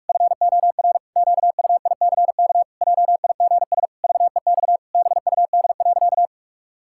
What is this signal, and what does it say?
Morse code, heard as a keyed tone, about 46 wpm, 700 Hz. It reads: FOR QUICK JIGS VEX BUD.
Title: Pangram Challenge v2 - 45wpm